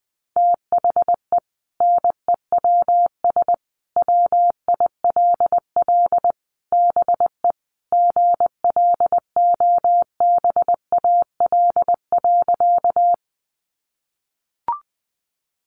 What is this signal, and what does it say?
Morse code, heard as a keyed tone, about 20 wpm, 700 Hz. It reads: THE NEWS WILL BE GLOBAL.
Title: Sentences from Top 1200 Words 20wpm